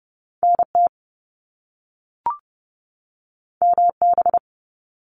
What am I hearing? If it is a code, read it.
Morse code, heard as a keyed tone, about 30 wpm, 700 Hz. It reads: NT MB
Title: Canadian Provinces and Territories 30wpm